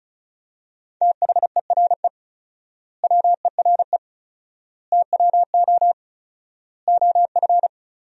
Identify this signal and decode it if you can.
Morse code, heard as a keyed tone, about 36 wpm, 700 Hz. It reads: THERE WERE TWO OF
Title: 4x Word-Spacing - Sentences from Top 100 Words - 35wpm